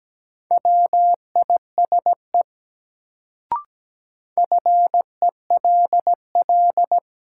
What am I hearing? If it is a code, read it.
Morse code, heard as a keyed tone, about 17 wpm, 700 Hz. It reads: WISE FELL
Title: Sentences from Top 1600 Words - New Word Drill - 17wpm